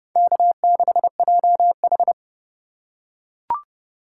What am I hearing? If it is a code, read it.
Morse code, heard as a keyed tone, about 31 wpm, 700 Hz. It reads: K6JH